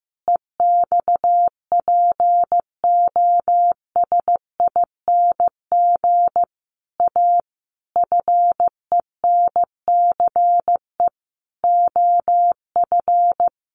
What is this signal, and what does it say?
Morse code, heard as a keyed tone, about 15 wpm, 700 Hz. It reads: EXPOSING A FENCE OF
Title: A Sound Of Thunder 15wpm